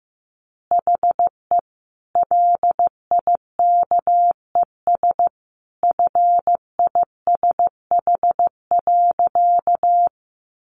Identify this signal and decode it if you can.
Morse code, heard as a keyed tone, about 15 wpm, 700 Hz. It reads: HE LIKES FISH.
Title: Additional Easy Phrases 15wpm